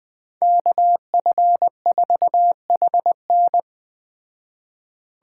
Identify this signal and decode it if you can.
Morse code, heard as a keyed tone, about 20 wpm, 700 Hz. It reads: KF4HN
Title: Speed Racing - USA Call Signs 17wpm